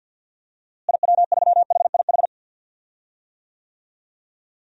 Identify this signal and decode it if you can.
Morse code, heard as a keyed tone, about 47 wpm, 700 Hz. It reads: IW3HIH
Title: DXCC-100 Call Signs - 6 Characters - Repeated 2x - 50wpm